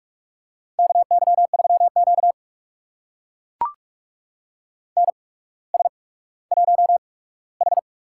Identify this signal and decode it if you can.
Morse code, heard as a keyed tone, about 44 wpm, spaced as for 19 wpm, 700 Hz. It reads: KY3Q NS1H
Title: USA Call Signs - 4 Characters - 8x Character Spacing - 45wpm